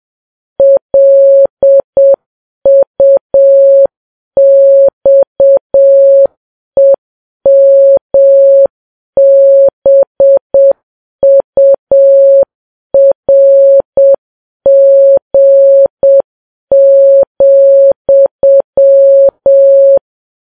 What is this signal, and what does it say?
Morse code, heard as a keyed tone, about 7 wpm, 550 Hz. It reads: LUXEMBURG,